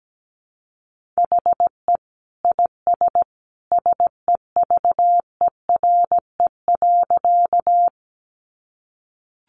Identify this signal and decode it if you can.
Morse code, heard as a keyed tone, about 17 wpm, 700 Hz. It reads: HE IS SEVERE.
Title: Sentences from Top 1400 Words 17wpm